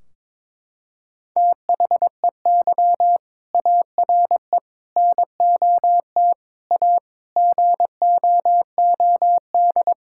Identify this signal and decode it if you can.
Morse code, heard as a keyed tone, about 21 wpm, 700 Hz. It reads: THEY ARE NOT A GOOD
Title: Sentences from Top 1600 Words 22wpm